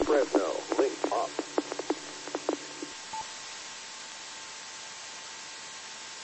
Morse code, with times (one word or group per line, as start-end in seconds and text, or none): none